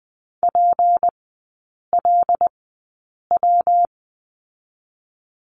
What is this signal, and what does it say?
Morse code, heard as a keyed tone, about 20 wpm, spaced as for 8 wpm, 700 Hz. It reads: PLW
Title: 152 Lesson 25e - Letter p - Three Characters 20wpm